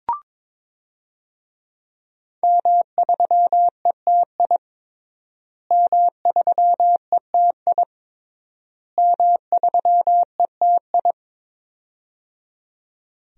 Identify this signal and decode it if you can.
Morse code, heard as a keyed tone, about 22 wpm, 700 Hz. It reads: M3ETI M3ETI M3ETI
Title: DXCC-100 Call Signs - 5 Characters - Repeated 3x - 22wpm